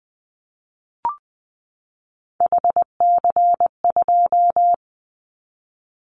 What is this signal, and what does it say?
Morse code, heard as a keyed tone, about 20 wpm, 700 Hz. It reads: HC2
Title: Callsign Prefixes for Top 100 DXCC - 20wpm